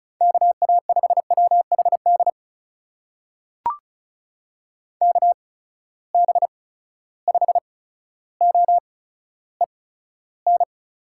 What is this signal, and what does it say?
Morse code, heard as a keyed tone, about 36 wpm, spaced as for 15 wpm, 700 Hz. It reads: KA5WHD KB5OEN
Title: USA Call Signs - 6 Characters - 8x Character Spacing - 35wpm